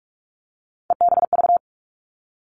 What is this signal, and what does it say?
Morse code, heard as a keyed tone, about 46 wpm, 700 Hz. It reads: EBV
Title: Three Challenging Letter-Number Combinations 45wpm